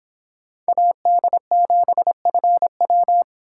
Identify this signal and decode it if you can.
Morse code, heard as a keyed tone, about 26 wpm, 700 Hz. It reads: AD7FW